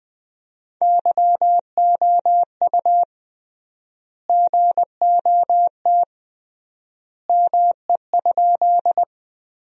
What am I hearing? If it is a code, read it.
Morse code, heard as a keyed tone, about 20 wpm, 700 Hz. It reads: YOU GOT ME?
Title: 3x Word-Spacing - Sentences from Top 300 Words - 20wpm